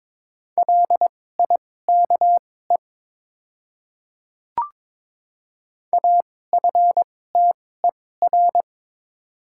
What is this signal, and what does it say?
Morse code, heard as a keyed tone, about 22 wpm, spaced as for 16 wpm, 690 Hz. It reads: LIKE AFTER
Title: 2x Character Spacing - Top 300 Words - 22wpm